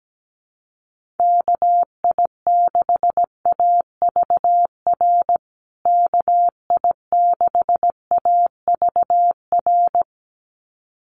Morse code, heard as a keyed tone, about 17 wpm, 700 Hz. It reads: KI6AVR KI6AVR
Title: USA Call Signs Repeated 2x 17wpm